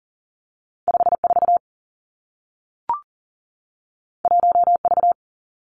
Morse code, heard as a keyed tone, about 42 wpm, 700 Hz. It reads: 5 4 1 V
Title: Two Challenging Letter-Number Combinations 40wpm